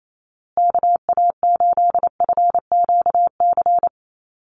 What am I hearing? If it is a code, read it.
Morse code, heard as a keyed tone, about 28 wpm, 700 Hz. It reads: KA8FQC
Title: USA Call Signs Repeated 3x 28wpm